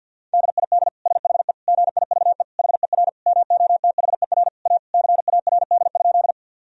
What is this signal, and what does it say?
Morse code, heard as a keyed tone, about 52 wpm, 700 Hz. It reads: DID SHE GIVE HER MOTHER A CARD?
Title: Sentences from Top 1000 Words - Review Part 4 - 50wpm